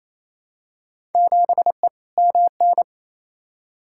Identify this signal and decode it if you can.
Morse code, heard as a keyed tone, about 28 wpm, 700 Hz. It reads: 7E MN
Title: Field Day Exchanges Encore - 28wpm